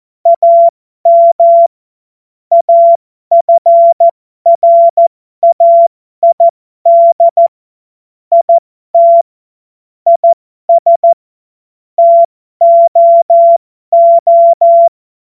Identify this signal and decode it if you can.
Morse code, heard as a keyed tone, about 14 wpm, spaced as for 12 wpm, 680 Hz. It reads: AM AFRAID IT IS TOO
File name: Morsekurs / Witze